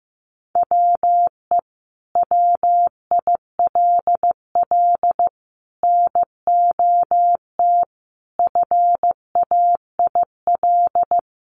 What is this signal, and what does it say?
Morse code, heard as a keyed tone, about 15 wpm, 700 Hz. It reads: WE WILL NOT FAIL
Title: Sentences from Top 1100 Words 15wpm